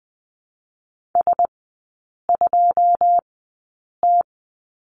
Morse code, heard as a keyed tone, about 20 wpm, spaced as for 8 wpm, 700 Hz. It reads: S2T
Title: 100 Lesson 17e - Letter u - Three Characters 20wpm